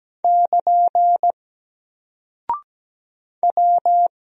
Morse code, heard as a keyed tone, about 17 wpm, 700 Hz. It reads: <KN> W
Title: Single Character (Letter:Number:Punctuation:Prosign) 17wpm